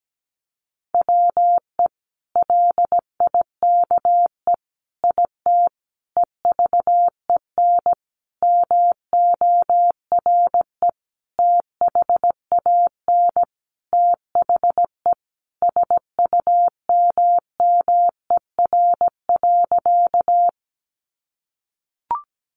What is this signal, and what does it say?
Morse code, heard as a keyed tone, about 17 wpm, 700 Hz. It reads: WE LIKE IT EVEN MORE THAN THE SUMMER.
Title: Sentences from Top 1100 Words 17wpm